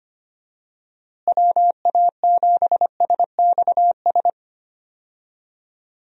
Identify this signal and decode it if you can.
Morse code, heard as a keyed tone, about 25 wpm, 700 Hz. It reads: WA7SXS
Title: USA Call Signs - 6 Characters - Repeated 3x - 25wpm